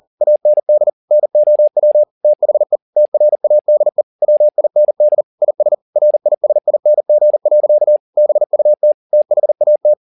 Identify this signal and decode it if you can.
Morse code, heard as a keyed tone, about 40 wpm, 600 Hz. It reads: AND NOW THE TRADE WIND IS RISING. BUT THAT
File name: the old man and the sea 2 40wpm